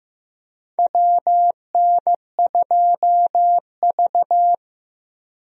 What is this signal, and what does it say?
Morse code, heard as a keyed tone, about 15 wpm, 700 Hz. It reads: WN2V
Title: USA Call Signs - 4 Characters - Repeated 3x - 15wpm